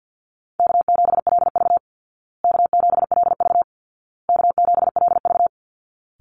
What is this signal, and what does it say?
Morse code, heard as a keyed tone, about 50 wpm, 700 Hz. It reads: K7LV K7LV K7LV
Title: Call Signs Repeated 2x - 50wpm